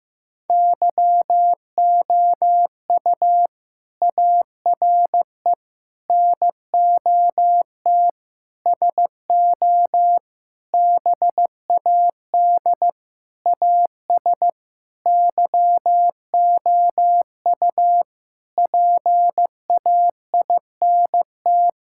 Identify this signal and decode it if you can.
Morse code, heard as a keyed tone, about 16 wpm, 700 Hz. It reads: YOU ARE NOT SO BAD AS YOU PAINT
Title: Sentences from Top 1000 Words - Review Part 6 - 15wpm